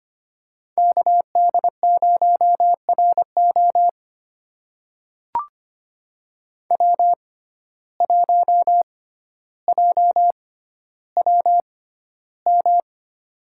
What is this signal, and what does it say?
Morse code, heard as a keyed tone, about 25 wpm, spaced as for 13 wpm, 700 Hz. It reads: KD0RO W1JWM
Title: USA Call Signs - 5 Characters - 6x Character Spacing - 25wpm